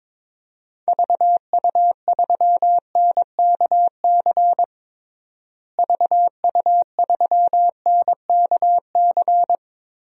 Morse code, heard as a keyed tone, about 22 wpm, 700 Hz. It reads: VU3NKC VU3NKC
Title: DXCC-100 Call Signs - 6 Characters - Repeated 3x - 22wpm